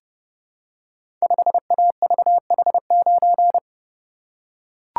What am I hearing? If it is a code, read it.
Morse code, heard as a keyed tone, about 30 wpm, 700 Hz. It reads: 5AVH9
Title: Code Groups - 5 Letter:Numbers - 30wpm